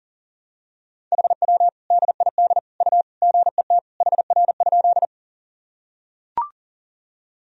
Morse code, heard as a keyed tone, about 40 wpm, 700 Hz. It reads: HW DID U GET HR?
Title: Sentences from Top 200 Words and Abbreviations 40wpm